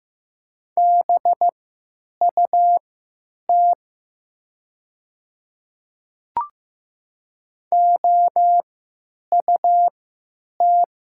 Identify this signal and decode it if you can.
Morse code, heard as a keyed tone, about 15 wpm, spaced as for 9 wpm, 700 Hz. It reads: BUT OUT
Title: 3x Character Spacing - Top 500 Words - 15wpm